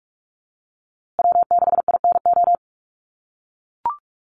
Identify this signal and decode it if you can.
Morse code, heard as a keyed tone, about 44 wpm, 690 Hz. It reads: W6INO
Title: USA Call Signs Repeated 2x 45wpm